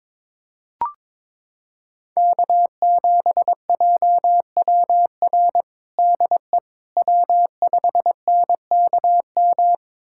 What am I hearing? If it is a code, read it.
Morse code, heard as a keyed tone, about 22 wpm, 700 Hz. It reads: K7JWR DE W5NKM
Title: USA Callsign DE Callsign 22wpm